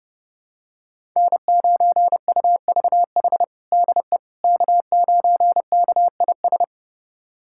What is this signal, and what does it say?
Morse code, heard as a keyed tone, about 32 wpm, 700 Hz. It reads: N9UVH DE K9KIS
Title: USA Callsign DE Callsign - 5 Characters - 30wpm